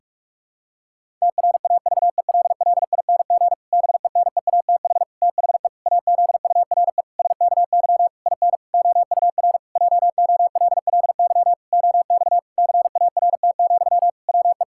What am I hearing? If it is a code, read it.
Morse code, heard as a keyed tone, about 41 wpm, 700 Hz. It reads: TRAVELLING BENEATH THE AZURE SKY IN OUR JOLLY OX CART, WE
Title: Pangram Challenge v2 - 45wpm